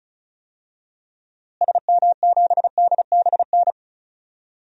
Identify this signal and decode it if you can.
Morse code, heard as a keyed tone, about 36 wpm, 700 Hz. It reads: SM7DBN
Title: DXCC-100 Call Signs - 6 Characters - Repeated 3x - 35wpm